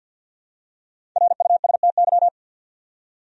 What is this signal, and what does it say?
Morse code, heard as a keyed tone, about 52 wpm, 700 Hz. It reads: RUSTY
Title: USA Common Names - Max 5 Letters - 50wpm